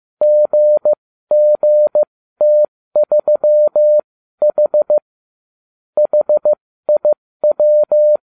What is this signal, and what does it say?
Morse code, heard as a keyed tone, about 15 wpm, spaced as for 13 wpm, 600 Hz. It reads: GGT3H HIW